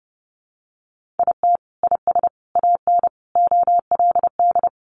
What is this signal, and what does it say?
Morse code, heard as a keyed tone, about 30 wpm, 700 Hz. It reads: IT IS AN OLD
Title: Sentences from Top 1500 Words 30wpm